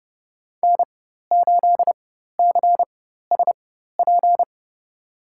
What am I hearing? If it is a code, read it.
Morse code, heard as a keyed tone, about 31 wpm, spaced as for 14 wpm, 700 Hz. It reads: N8CSP
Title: USA Call Signs - 5 Characters - 4x Character Spacing - 30wpm